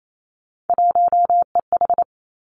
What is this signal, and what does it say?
Morse code, heard as a keyed tone, about 28 wpm, 700 Hz. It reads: 1EH
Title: Three Challenging Letter-Number Combinations 28wpm